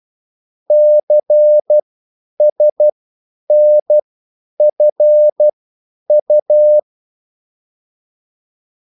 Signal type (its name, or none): Morse code